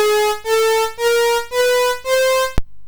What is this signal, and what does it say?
There is no code.